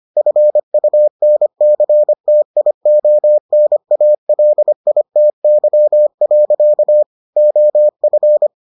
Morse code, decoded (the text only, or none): FUNCTIONALITY. OF